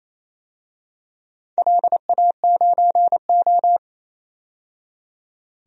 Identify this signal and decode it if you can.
Morse code, heard as a keyed tone, about 28 wpm, 700 Hz. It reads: LA9O
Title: DXCC-100 Call Signs - 4 Characters - 28wpm